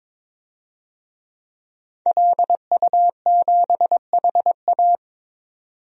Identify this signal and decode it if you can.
Morse code, heard as a keyed tone, about 22 wpm, 700 Hz. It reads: LU7HA
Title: DXCC-100 Call Signs - 5 Characters - 22wpm